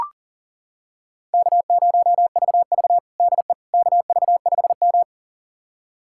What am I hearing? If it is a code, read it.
Morse code, heard as a keyed tone, about 40 wpm, 700 Hz. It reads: K0VV DE KV5M
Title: USA Callsign DE Callsign - 4 Characters - 40wpm